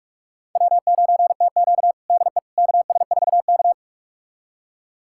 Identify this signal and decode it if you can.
Morse code, heard as a keyed tone, about 44 wpm, 690 Hz. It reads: W9TQ DE KS4K
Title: USA Callsign DE Callsign - 4 Characters - 45wpm